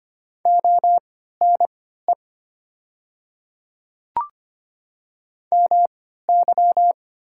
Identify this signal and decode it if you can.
Morse code, heard as a keyed tone, about 25 wpm, spaced as for 14 wpm, 690 Hz. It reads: ONE MY
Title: 3x Character Spacing - Top 100 Words - 25wpm